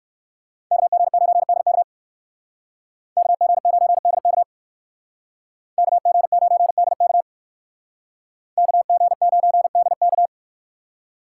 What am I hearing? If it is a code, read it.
Morse code, heard as a keyed tone, about 56 wpm, 700 Hz. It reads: KG9DK KG9DK KG9DK KG9DK
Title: Speed Racing - USA 2x2 Call Signs 45wpm